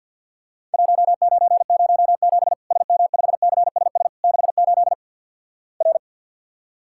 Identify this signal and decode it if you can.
Morse code, heard as a keyed tone, about 50 wpm, 700 Hz. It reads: 1907 SM5CSS 68 R
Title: Worked All Europe DX Contest - QTC Exchange - 50wpm